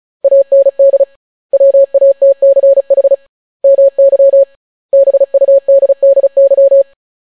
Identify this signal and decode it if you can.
Morse code, heard as a keyed tone, about 36 wpm, 550 Hz. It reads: AND WATCH MY BUDDY